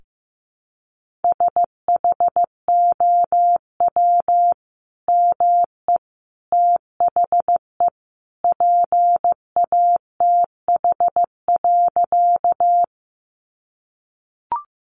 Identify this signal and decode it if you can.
Morse code, heard as a keyed tone, about 15 wpm, 700 Hz. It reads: SHOW ME THE PATH.
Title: Sentences from Top 1400 Words 15wpm